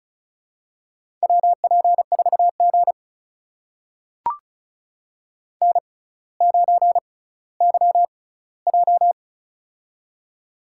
Morse code, heard as a keyed tone, about 34 wpm, spaced as for 18 wpm, 700 Hz. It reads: WP4G N9YJ
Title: USA Call Signs - 4 Characters - 6x Character Spacing - 35wpm